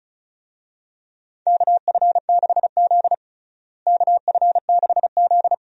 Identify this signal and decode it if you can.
Morse code, heard as a keyed tone, about 36 wpm, 700 Hz. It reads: KF6Z KF6Z
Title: USA Call Signs - 4 Characters - Repeated 2x - 35wpm